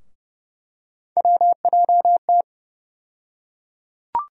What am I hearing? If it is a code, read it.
Morse code, heard as a keyed tone, about 32 wpm, 700 Hz. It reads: WJT
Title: USA Call Sign Postfixes - 3 Letters 30wpm